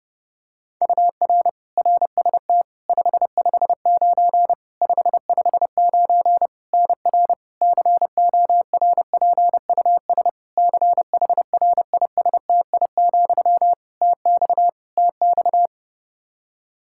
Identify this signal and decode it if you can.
Morse code, heard as a keyed tone, about 28 wpm, 700 Hz. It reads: UR RST 559 559 NR CORPUS CHRISTI, TX TX
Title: K1USN SST QSO Practice 30wpm